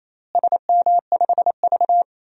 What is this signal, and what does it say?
Morse code, heard as a keyed tone, about 28 wpm, 700 Hz. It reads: SM5V
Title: DXCC-100 Call Signs - 4 Characters - Repeated 2x - 28wpm